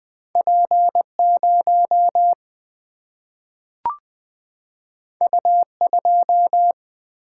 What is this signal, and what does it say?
Morse code, heard as a keyed tone, about 20 wpm, 700 Hz. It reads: P 0 U 2
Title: Single-Letter Single-Number 20wpm